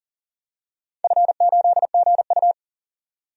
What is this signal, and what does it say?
Morse code, heard as a keyed tone, about 40 wpm, 700 Hz. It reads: F8GU
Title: DXCC-100 Call Signs - 4 Characters - 40wpm